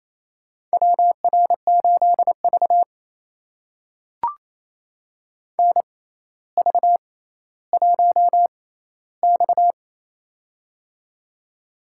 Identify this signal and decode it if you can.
Morse code, heard as a keyed tone, about 27 wpm, spaced as for 14 wpm, 700 Hz. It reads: WR8V NV1X
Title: USA Call Signs - 4 Characters - 6x Character Spacing - 28wpm